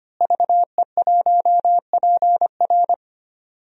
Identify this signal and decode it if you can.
Morse code, heard as a keyed tone, about 25 wpm, 700 Hz. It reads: VE1PR
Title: DXCC-100 Call Signs - 5 Characters - Repeated 3x - 25wpm